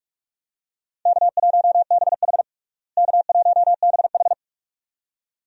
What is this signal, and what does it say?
Morse code, heard as a keyed tone, about 46 wpm, 700 Hz. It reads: K1BH K1BH
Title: USA Call Signs - 4 Characters - Repeated 2x - 45wpm